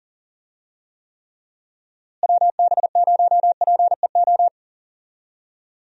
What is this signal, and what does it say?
Morse code, heard as a keyed tone, about 40 wpm, 700 Hz. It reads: WB0PEO
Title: USA Call Signs - 6 Characters - Repeated 2x - 40wpm